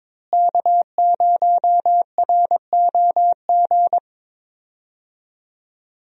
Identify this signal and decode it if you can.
Morse code, heard as a keyed tone, about 22 wpm, 700 Hz. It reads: K0ROG
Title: USA Call Signs - 5 Characters - Repeated 3x - 22wpm